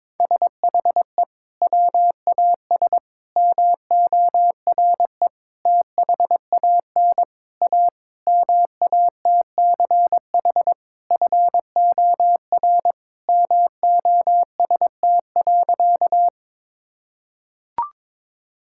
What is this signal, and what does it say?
Morse code, heard as a keyed tone, about 22 wpm, 700 Hz. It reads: SHE WAS MORE THAN A MATCH FOR MOST.
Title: Sentences from Top 1600 Words 22wpm